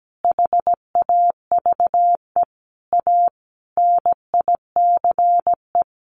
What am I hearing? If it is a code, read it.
Morse code, heard as a keyed tone, about 17 wpm, 700 Hz. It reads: HAVE A NICE
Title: Sentences from Top 1100 Words 17wpm